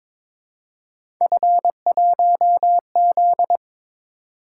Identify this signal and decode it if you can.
Morse code, heard as a keyed tone, about 23 wpm, 700 Hz. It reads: F1Z
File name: practice / 22wpm / Three Letter-Number Combinations - 22wpm